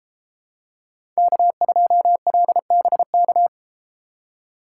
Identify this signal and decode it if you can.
Morse code, heard as a keyed tone, about 32 wpm, 700 Hz. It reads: K2LBK